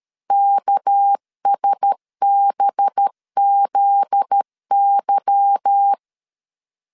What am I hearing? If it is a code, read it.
Morse code, heard as a keyed tone, about 13 wpm, 800 Hz. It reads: KSBZY
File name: LEZIONI / 8 YJ45 / 8 Y 12wpm 2228